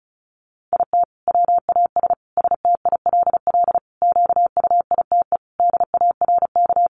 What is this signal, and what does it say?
Morse code, heard as a keyed tone, about 34 wpm, 700 Hz. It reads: IT WAS STILL QUITE DARK
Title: Sentences from Top 900 Words 35wpm